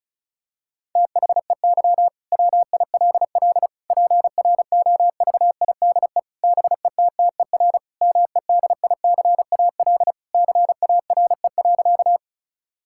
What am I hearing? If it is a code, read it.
Morse code, heard as a keyed tone, about 36 wpm, 700 Hz. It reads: THEY WILL PROVIDE BETTER MEDICAL CARE.
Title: Sentences from Top 1000 Words - Review Part 5 - 35wpm